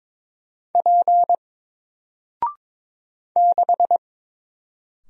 Morse code, heard as a keyed tone, about 22 wpm, 700 Hz. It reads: P 6
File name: Single-Letter-Number-22wpm